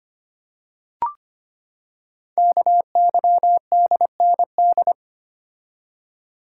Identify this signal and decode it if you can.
Morse code, heard as a keyed tone, about 25 wpm, 700 Hz. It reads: KYDND
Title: Code Groups - 5 Letters - 25wpm